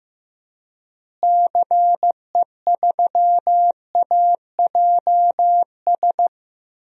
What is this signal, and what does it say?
Morse code, heard as a keyed tone, about 15 wpm, 700 Hz. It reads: CE3AJS
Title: DXCC-100 Call Signs - 6 Characters - Repeated 3x - 15wpm